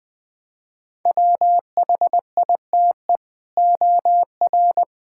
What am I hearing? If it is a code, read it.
Morse code, heard as a keyed tone, about 21 wpm, 700 Hz. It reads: WHITE OR
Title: Sentences from Top 1600 Words 20wpm